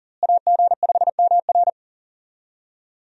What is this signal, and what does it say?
Morse code, heard as a keyed tone, about 40 wpm, 700 Hz. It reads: AG5MR
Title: USA Call Signs - 5 Characters - Repeated 3x - 40wpm